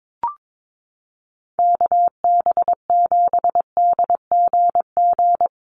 Morse code, heard as a keyed tone, about 22 wpm, 700 Hz. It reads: KB7DGG
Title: USA Call Signs Repeated 2x 22wpm